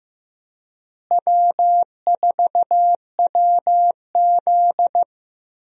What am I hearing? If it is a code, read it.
Morse code, heard as a keyed tone, about 15 wpm, 700 Hz. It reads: W4WZ
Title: USA Call Signs - 4 Characters - Repeated 3x - 15wpm